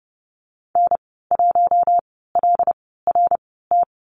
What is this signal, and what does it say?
Morse code, heard as a keyed tone, about 30 wpm, spaced as for 17 wpm, 700 Hz. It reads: N1LRT
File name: Lesson 14g - Number 2 - Call Signs 30wpm